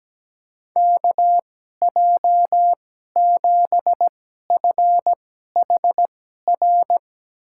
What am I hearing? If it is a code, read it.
Morse code, heard as a keyed tone, about 17 wpm, spaced as for 12 wpm, 700 Hz. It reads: KJ7FHR
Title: USA Call Signs - 6 Characters - 2x Character Spacing - 17wpm